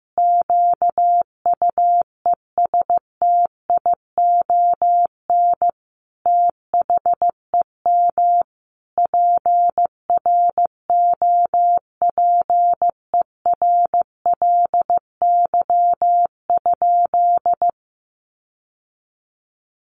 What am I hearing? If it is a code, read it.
Morse code, heard as a keyed tone, about 14 wpm, 700 Hz. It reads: QUESTION THEM PROPERLY?
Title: Sentences from Top 1100 Words 15wpm